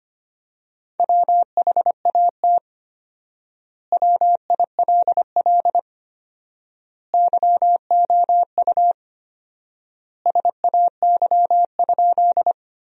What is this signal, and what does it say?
Morse code, heard as a keyed tone, about 25 wpm, 700 Hz. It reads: WHAT WILL YOU SAY?
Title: 4x Word-Spacing - Sentences from Top 200 Words - 25wpm